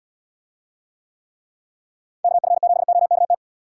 Morse code, heard as a keyed tone, about 77 wpm, 700 Hz. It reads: KF7OKT